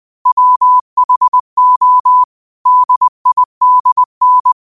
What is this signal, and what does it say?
Morse code, heard as a keyed tone, about 20 wpm, 1000 Hz. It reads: WHO DIDN
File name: qotdfiles / 20wpm